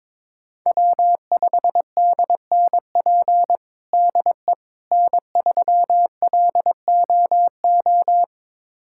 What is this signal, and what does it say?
Morse code, heard as a keyed tone, about 22 wpm, 700 Hz. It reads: W5DNP DE N3LOO
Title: USA Callsign DE Callsign - 5 Characters - 22wpm